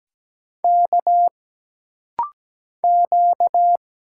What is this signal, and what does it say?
Morse code, heard as a keyed tone, about 17 wpm, 700 Hz. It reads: K Q
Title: Single Letter-Number Alt - Rapid-Fire 17wpm